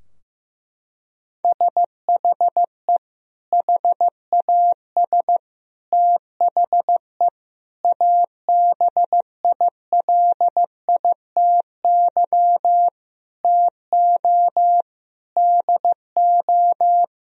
Morse code, heard as a keyed tone, about 15 wpm, 700 Hz. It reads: SHE HAS THE ABILITY TO DO
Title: Sentences from Top 1000 Words - Review Part 3 - 15wpm